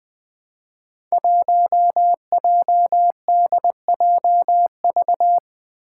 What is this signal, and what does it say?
Morse code, heard as a keyed tone, about 20 wpm, 700 Hz. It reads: 1JDJV
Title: Five Challenging Letter-Number Combinations 20wpm